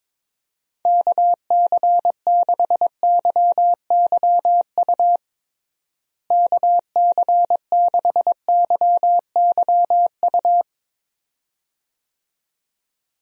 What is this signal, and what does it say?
Morse code, heard as a keyed tone, about 21 wpm, 700 Hz. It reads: KC6YYU KC6YYU
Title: USA Call Signs - 6 Characters - Repeated 2x - 22wpm